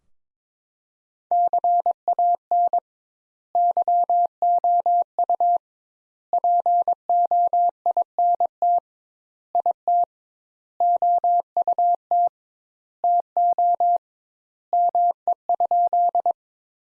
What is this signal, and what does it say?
Morse code, heard as a keyed tone, about 22 wpm, 700 Hz. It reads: CAN YOU POINT IT OUT TO ME?
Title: Sentences Top 200 Words - 22wpm